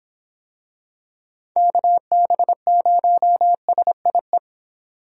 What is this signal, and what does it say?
Morse code, heard as a keyed tone, about 26 wpm, 700 Hz. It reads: KB0SIE